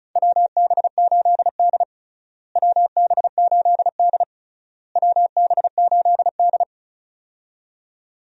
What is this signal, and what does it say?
Morse code, heard as a keyed tone, about 34 wpm, 700 Hz. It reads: WB8D WB8D WB8D
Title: USA Call Signs - 4 Characters - Repeated 3x - 35wpm